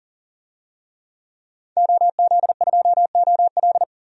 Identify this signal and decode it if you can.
Morse code, heard as a keyed tone, about 40 wpm, 700 Hz. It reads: OZ2OL
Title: DXCC-100 Call Signs - 5 Characters - 40wpm